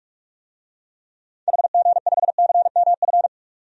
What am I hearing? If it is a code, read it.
Morse code, heard as a keyed tone, about 41 wpm, 700 Hz. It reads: HG5CMF
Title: DXCC-100 Call Signs - 6 Characters - 45wpm